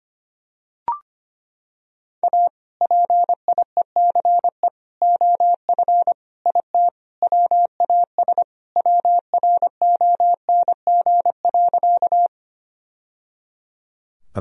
Morse code, heard as a keyed tone, about 25 wpm, 700 Hz. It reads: A PIECE OF IT WAS WRONG.
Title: Sentences from Top 1000 Words - Review Part 2 - 25wpm